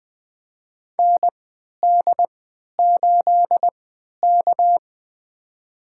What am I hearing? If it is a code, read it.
Morse code, heard as a keyed tone, about 20 wpm, spaced as for 11 wpm, 700 Hz. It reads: ND8K
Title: USA Call Signs - 3x Character Spacing - 20wpm